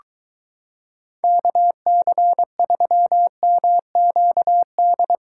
Morse code, heard as a keyed tone, about 22 wpm, 700 Hz. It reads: KC3MQD